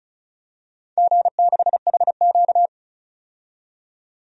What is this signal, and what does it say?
Morse code, heard as a keyed tone, about 36 wpm, 700 Hz. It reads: G6HQ
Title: DXCC-100 Call Signs - 4 Characters - 35wpm